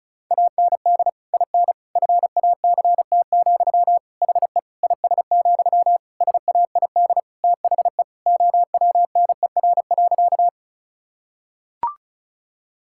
Morse code, heard as a keyed tone, about 34 wpm, 700 Hz. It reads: AND IN FACT, HE IS, SAID THE OWNER.
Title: Sentences from Top 1600 Words 35wpm